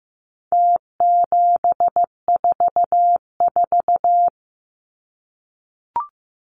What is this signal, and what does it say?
Morse code, heard as a keyed tone, about 15 wpm, 700 Hz. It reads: T744
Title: Numbers - Contest Style - 4-Digit 15wpm